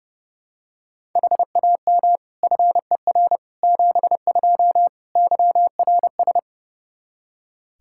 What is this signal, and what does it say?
Morse code, heard as a keyed tone, about 32 wpm, 700 Hz. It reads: HAM FER 72 YRS
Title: K1USN SST QSO Practice 30wpm